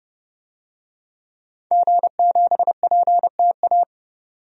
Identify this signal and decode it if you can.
Morse code, heard as a keyed tone, about 30 wpm, 700 Hz. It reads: G7PTA
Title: DXCC-100 Call Signs - 5 Characters - 30wpm